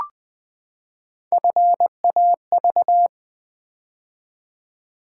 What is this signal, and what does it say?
Morse code, heard as a keyed tone, about 20 wpm, 700 Hz. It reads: FAV
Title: Build A Word - 3 Letters At A Time - 9-Letter Words - 20wpm